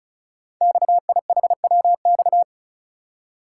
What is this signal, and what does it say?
Morse code, heard as a keyed tone, about 36 wpm, 690 Hz. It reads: XIHWX
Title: Code Groups - 5 Letters - 35wpm